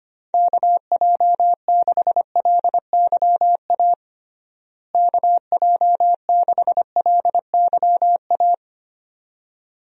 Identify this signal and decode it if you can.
Morse code, heard as a keyed tone, about 25 wpm, 700 Hz. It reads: KJ6LYA KJ6LYA
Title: USA Call Signs - 6 Characters - Repeated 2x - 25wpm